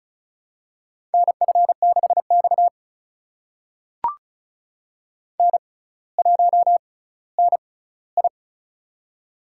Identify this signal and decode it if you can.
Morse code, heard as a keyed tone, about 36 wpm, spaced as for 18 wpm, 700 Hz. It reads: NF6X N1NI
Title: USA Call Signs - 4 Characters - 6x Character Spacing - 35wpm